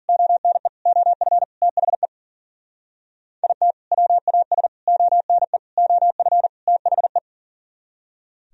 Morse code, heard as a keyed tone, about 50 wpm, 700 Hz. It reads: ONE OF THE IT WAS ONE OF THE
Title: Speed-Racing - Sets of 5 Words 40wpm